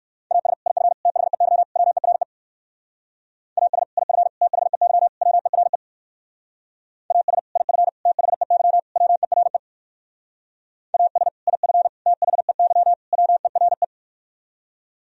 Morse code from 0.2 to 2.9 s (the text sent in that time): AS IF THEY WERE